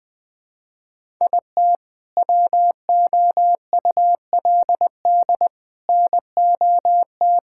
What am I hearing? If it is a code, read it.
Morse code, heard as a keyed tone, about 21 wpm, 700 Hz. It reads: IT WOULD NOT
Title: Sentences with 3-Word Phrases - 20wpm